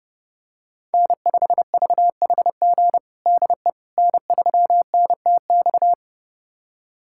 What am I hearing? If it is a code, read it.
Morse code, heard as a keyed tone, about 30 wpm, 700 Hz. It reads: N5VHG DE N3NTX
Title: USA Callsign DE Callsign - 5 Characters - 30wpm